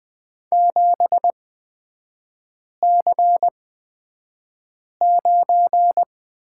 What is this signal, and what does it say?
Morse code, heard as a keyed tone, about 20 wpm, spaced as for 5 wpm, 700 Hz. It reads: 7C9
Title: 255 Lesson 41e - Period - Three Characters 20wpm